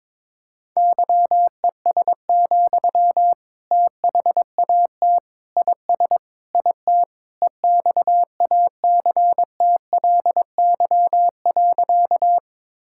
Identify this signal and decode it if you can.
Morse code, heard as a keyed tone, about 21 wpm, 700 Hz. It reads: YES, THAT IS IT EXACTLY.
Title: Sentences from Top 1000 Words - Review Part 6 - 22wpm